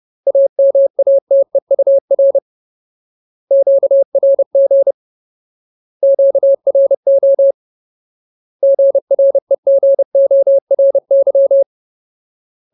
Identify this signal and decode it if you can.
Morse code, heard as a keyed tone, about 30 wpm, 550 Hz. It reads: AMATEUR QRG QRO GREGORY
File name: Ultimate Word List / 30WPM Ultimate